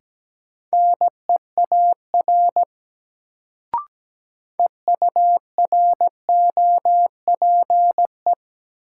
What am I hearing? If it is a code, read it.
Morse code, heard as a keyed tone, about 17 wpm, 700 Hz. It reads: NEAR EUROPE
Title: Sets of 5 Words Encore - New Word Drill - 17wpm